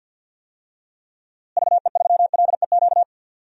Morse code, heard as a keyed tone, about 47 wpm, 700 Hz. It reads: VE3LEQ
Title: DXCC-100 Call Signs - 6 Characters - Repeated 3x - 50wpm